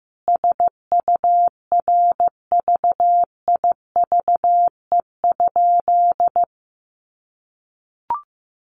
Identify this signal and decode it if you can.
Morse code, heard as a keyed tone, about 15 wpm, 700 Hz. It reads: SURVIVE?
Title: A Sound Of Thunder 15wpm